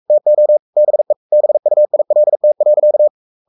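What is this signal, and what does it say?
Morse code, heard as a keyed tone, about 44 wpm, 600 Hz. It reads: TO BE BUILT.
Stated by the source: Morse code at 43 wpm sending Ken Follett's novel Never in English, from CW med Gnister.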